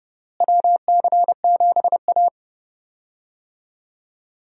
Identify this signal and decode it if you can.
Morse code, heard as a keyed tone, about 30 wpm, 700 Hz. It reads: WC7A
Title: USA Call Signs - 4 Characters - Repeated 2x - 30wpm